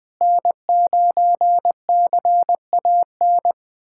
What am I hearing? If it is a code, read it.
Morse code, heard as a keyed tone, about 21 wpm, 700 Hz. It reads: N9CAN
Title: USA Callsign DE Callsign - 5 Characters - 20wpm